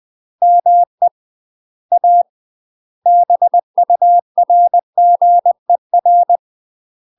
Morse code, heard as a keyed tone, about 20 wpm, 700 Hz. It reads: ME A BURGER
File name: UK MASTER QSOs TRAINING / 20WPM / UK MASTER QSOs 0073